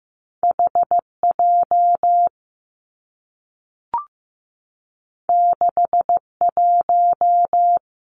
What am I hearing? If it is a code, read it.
Morse code, heard as a keyed tone, about 15 wpm, 700 Hz. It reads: H J 6 1
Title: Two Challenging Letter-Number Combinations 15wpm